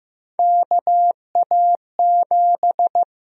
Morse code, heard as a keyed tone, about 15 wpm, 700 Hz. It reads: KA7
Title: USA Call Sign Prefixes - Repeated 2x - 15wpm